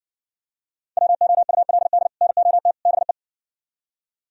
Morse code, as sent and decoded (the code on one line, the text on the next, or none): .-- --- ..- .-.. -.. / -. --- - / -... .
WOULD NOT BE